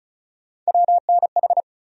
Morse code, heard as a keyed tone, about 34 wpm, 700 Hz. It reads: WNH
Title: USA Call Sign Postfixes - 3 Letters 35wpm